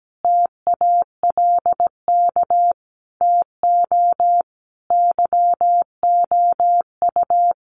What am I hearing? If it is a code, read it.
Morse code, heard as a keyed tone, about 17 wpm, 700 Hz. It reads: TALK TO YOU
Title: Sets of 5 Words Encore 17wpm